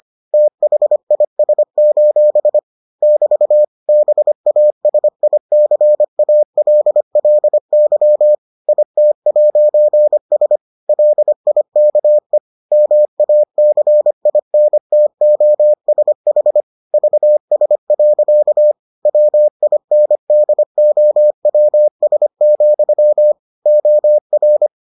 Morse code, heard as a keyed tone, about 25 wpm, 600 Hz. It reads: THIS: <BT> BASICALLY IT'S LIKE MACINTOSH VS. WINDOWS, OR